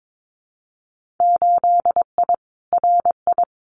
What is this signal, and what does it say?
Morse code, heard as a keyed tone, about 22 wpm, 700 Hz. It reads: 8I RI
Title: Winter Field Day Exchanges Encore 22wpm